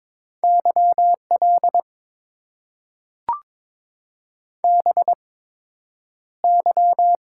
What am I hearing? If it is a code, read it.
Morse code, heard as a keyed tone, about 22 wpm, spaced as for 9 wpm, 700 Hz. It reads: YL BY
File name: Top 90 QSO Elements - 8x Character Spacing - 22wpm